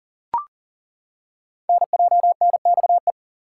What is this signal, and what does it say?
Morse code, heard as a keyed tone, about 38 wpm, 690 Hz. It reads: NJNXE
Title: Code Groups - 5 Letters - 40wpm